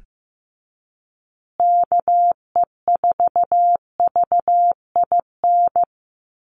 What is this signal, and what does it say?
Morse code, heard as a keyed tone, about 15 wpm, 700 Hz. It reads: KE4VIN
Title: USA Call Signs Repeated 3x 15wpm